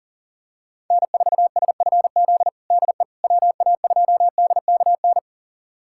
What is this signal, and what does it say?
Morse code, heard as a keyed tone, about 40 wpm, 700 Hz. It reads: N4SFZ DE WA2DKN
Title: USA Callsign DE Callsign 40wpm